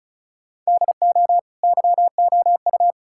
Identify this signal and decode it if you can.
Morse code, heard as a keyed tone, about 36 wpm, 700 Hz. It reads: DO YOU
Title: Sentences from Top 1000 Words - Review Part 6 - 35wpm